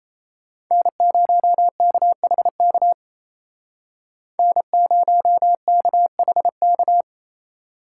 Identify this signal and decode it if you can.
Morse code, heard as a keyed tone, about 36 wpm, 700 Hz. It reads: N0KHK N0KHK